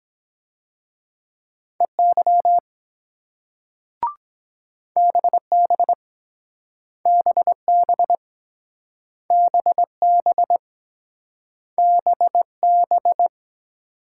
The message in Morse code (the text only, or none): EY BB BB BB BB